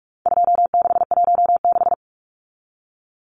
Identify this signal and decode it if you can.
Morse code, heard as a keyed tone, about 44 wpm, 700 Hz. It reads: 2616
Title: Numbers - 4-Digit 45wpm